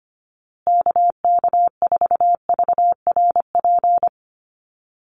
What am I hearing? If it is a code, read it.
Morse code, heard as a keyed tone, about 25 wpm, 700 Hz. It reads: KK4VRP
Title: USA Call Signs Repeated 3x 25wpm